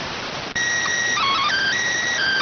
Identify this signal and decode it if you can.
SSTV (slow-scan television), PD 290.